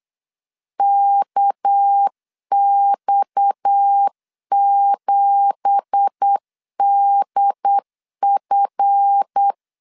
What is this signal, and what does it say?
Morse code, heard as a keyed tone, about 8 wpm, 800 Hz. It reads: KX7DF
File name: CW / DET / 1 9 1500C S C 8wpm 5653